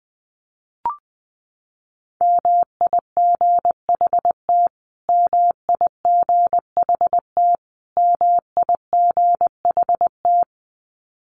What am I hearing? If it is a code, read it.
Morse code, heard as a keyed tone, about 20 wpm, 700 Hz. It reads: MIGHT MIGHT MIGHT
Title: Top 300 Words Repeated 20wpm